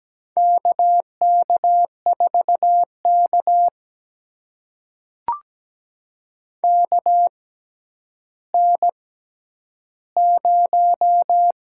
Morse code, heard as a keyed tone, about 17 wpm, spaced as for 10 wpm, 700 Hz. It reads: KK4K KN0
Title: USA Call Signs - 4 Characters - 6x Character Spacing - 17wpm